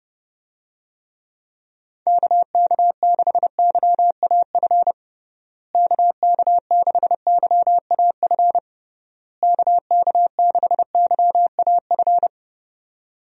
Morse code, heard as a keyed tone, about 32 wpm, 700 Hz. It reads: KK6YAF KK6YAF KK6YAF
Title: USA Call Signs - 6 Characters - Repeated 3x - 30wpm